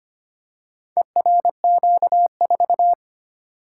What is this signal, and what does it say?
Morse code, heard as a keyed tone, about 26 wpm, 700 Hz. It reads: ERQ4
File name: practice / 25wpm / Four Letter-Number Combinations - 25wpm